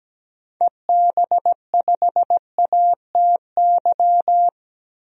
Morse code, heard as a keyed tone, about 17 wpm, 700 Hz. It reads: EB5ATY
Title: DXCC-100 Call Signs - 6 Characters - 17wpm